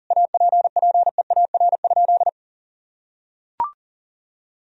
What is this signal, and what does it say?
Morse code, heard as a keyed tone, about 40 wpm, 700 Hz. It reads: APPEAR?
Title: Sentences from Top 1000 Words - Review Part 2 - 40wpm